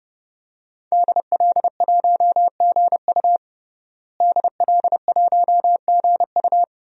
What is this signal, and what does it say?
Morse code, heard as a keyed tone, about 31 wpm, 700 Hz. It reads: DL1GU DL1GU
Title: DXCC-100 Call Signs - 5 Characters - Repeated 2x - 30wpm